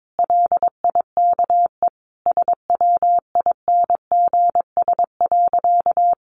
Morse code, heard as a keyed tone, about 22 wpm, 700 Hz. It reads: LIKE SWINGS.
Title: Easy Phrases 22wpm